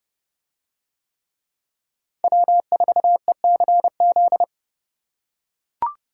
Morse code, heard as a keyed tone, about 30 wpm, 700 Hz. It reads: W4ECZ